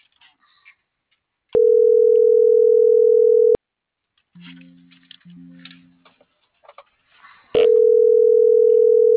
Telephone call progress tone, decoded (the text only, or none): ringback tone